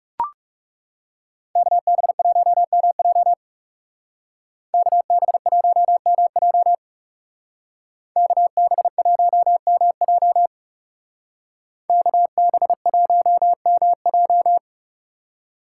Morse code, heard as a keyed tone, about 37 wpm, 700 Hz. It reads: KB1MJ KB1MJ KB1MJ KB1MJ
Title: Speed Racing - USA 2x2 Call Signs 30wpm